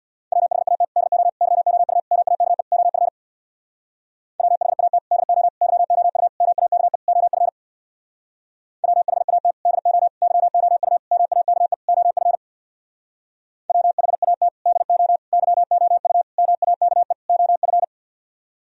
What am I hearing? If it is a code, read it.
Morse code, heard as a keyed tone, about 64 wpm, 700 Hz. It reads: WHAT DO YOU MAKE OF WHAT DO YOU MAKE OF WHAT DO YOU MAKE OF WHAT DO YOU MAKE OF